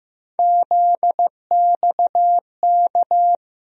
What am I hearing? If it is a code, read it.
Morse code, heard as a keyed tone, about 16 wpm, 700 Hz. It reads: ZXK
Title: USA Call Sign Postfixes - 3 Letters 15wpm